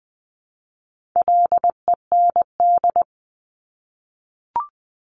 Morse code, heard as a keyed tone, about 20 wpm, 700 Hz. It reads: LEND
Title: 260 Lesson 42e - Words 20-20wpm